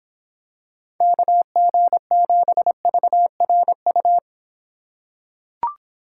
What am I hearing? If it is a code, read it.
Morse code, heard as a keyed tone, about 25 wpm, 700 Hz. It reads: KG7VRU